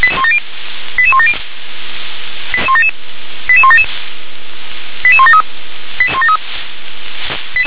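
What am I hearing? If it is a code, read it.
Five-tone selective call (ZVEI), heard as quick runs of five tones, tones of about 70 ms. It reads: 88170 88177 88173
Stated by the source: ZVEI five-tone selective call 88177, received on SDR.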